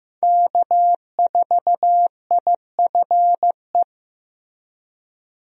Morse code, heard as a keyed tone, about 15 wpm, 700 Hz. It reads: K4IFE
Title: USA Call Signs - 5 Characters - Repeated 2x - 15wpm